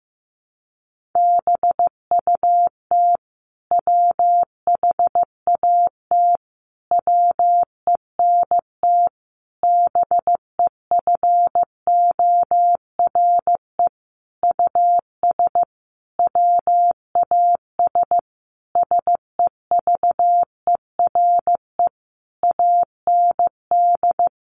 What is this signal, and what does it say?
Morse code, heard as a keyed tone, about 15 wpm, 700 Hz. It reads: BUT WHAT WENT BEFORE US WAS SEVERE AND
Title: Sentences from Top 1400 Words 15wpm